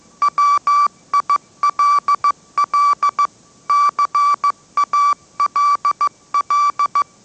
Morse code, heard as a keyed tone, about 15 wpm, 1160 Hz. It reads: WILL CALL